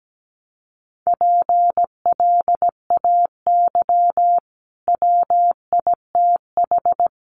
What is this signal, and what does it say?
Morse code, heard as a keyed tone, about 17 wpm, 700 Hz. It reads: PLAY WITH
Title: Sentences from Top 200 Words 17wpm